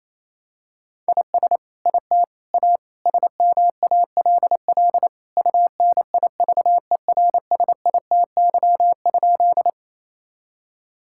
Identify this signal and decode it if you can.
Morse code, heard as a keyed tone, about 28 wpm, 700 Hz. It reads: IS IT A SMALL UNIVERSITY?
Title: Sentences from Top 1000 Words - Review Part 5 - 28wpm